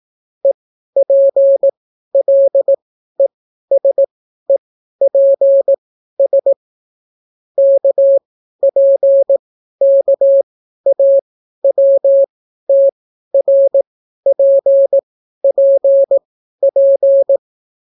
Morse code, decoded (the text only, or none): EPLESEPS KPKAWTRPPP